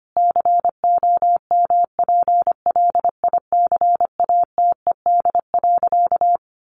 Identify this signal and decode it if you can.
Morse code, heard as a keyed tone, about 25 wpm, 700 Hz. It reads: COMPLICATED.
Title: Sentences from Top 1200 Words 25wpm